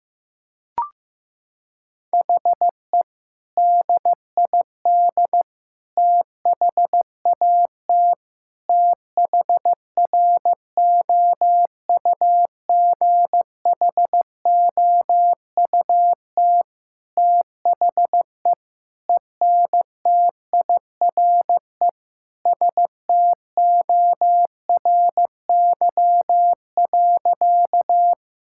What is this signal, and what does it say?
Morse code, heard as a keyed tone, about 16 wpm, 700 Hz. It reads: HE DID THAT THROUGHOUT THE ENTIRE STORY.
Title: Sentences from Top 1000 Words - Review Part 5 - 15wpm